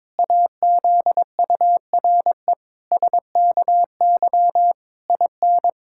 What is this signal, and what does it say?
Morse code, heard as a keyed tone, about 22 wpm, 700 Hz. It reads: AZURE SKY IN
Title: Pangram Challenge v2 - 22wpm